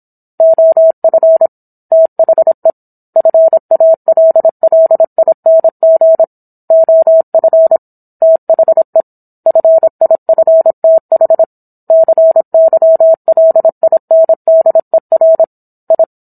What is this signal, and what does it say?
Morse code, heard as a keyed tone, about 27 wpm, 650 Hz. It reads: OF THE FALLING OF THE FIFTH CYLINDER I